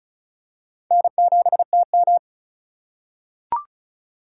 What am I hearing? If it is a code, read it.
Morse code, heard as a keyed tone, about 36 wpm, 700 Hz. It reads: N7TM
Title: USA Call Signs - 4 Characters - 4x Character Spacing - 35wpm